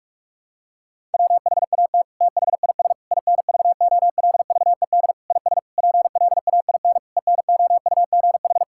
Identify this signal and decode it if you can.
Morse code, heard as a keyed tone, about 47 wpm, 700 Hz. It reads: WHAT THIS INVOLVED IS PLAIN ENOUGH
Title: Sentences from Top 1000 Words - Review Part 5 - 45wpm